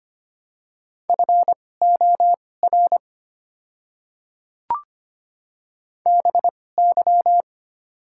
Morse code, heard as a keyed tone, about 25 wpm, spaced as for 18 wpm, 700 Hz. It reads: FOR BY
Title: 2x Character Spacing - Top 100 Words - 25wpm